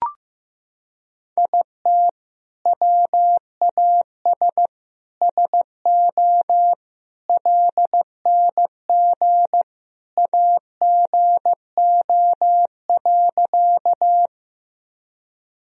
Morse code, heard as a keyed tone, about 15 wpm, 700 Hz. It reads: IT WAS SO LNG AGO.
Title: Sentences from Top 500 Words and Abbreviations 15wpm